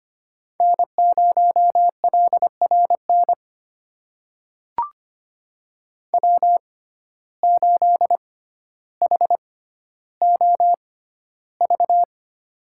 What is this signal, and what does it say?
Morse code, heard as a keyed tone, about 25 wpm, spaced as for 13 wpm, 700 Hz. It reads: N0LRN W8HOV
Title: USA Call Signs - 5 Characters - 6x Character Spacing - 25wpm